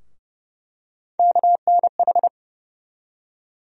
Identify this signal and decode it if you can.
Morse code, heard as a keyed tone, about 30 wpm, 700 Hz. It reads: KNH
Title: USA Call Sign Postfixes - 3 Letters 30wpm